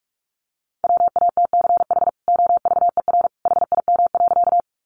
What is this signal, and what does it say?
Morse code, heard as a keyed tone, about 44 wpm, 700 Hz. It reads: WATCH OVER HIM.
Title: Sentences from Top 300 Words 45wpm